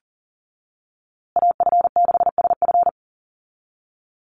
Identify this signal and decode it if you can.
Morse code, heard as a keyed tone, about 40 wpm, 700 Hz. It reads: AF6SF
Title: USA Call Signs Repeated 3x 40wpm